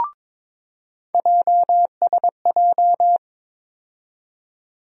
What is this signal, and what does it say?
Morse code, heard as a keyed tone, about 22 wpm, 700 Hz. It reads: JSJ
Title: USA Call Sign Postfixes - 3 Letters 22wpm